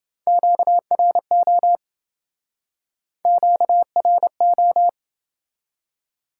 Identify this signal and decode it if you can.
Morse code, heard as a keyed tone, about 27 wpm, 700 Hz. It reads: QRO QRO